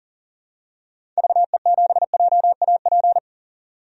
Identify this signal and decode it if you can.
Morse code, heard as a keyed tone, about 40 wpm, 700 Hz. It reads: VE7JAP
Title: DXCC-100 Call Signs - 6 Characters - 40wpm